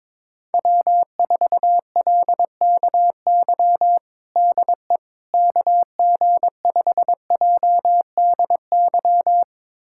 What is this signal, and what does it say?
Morse code, heard as a keyed tone, about 22 wpm, 700 Hz. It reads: W4LKY DE KG5JDY
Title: USA Callsign DE Callsign 22wpm